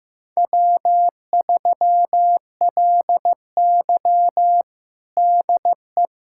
Morse code, heard as a keyed tone, about 15 wpm, 700 Hz. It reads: W3LY DE
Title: USA Callsign DE Callsign - 4 Characters - 15wpm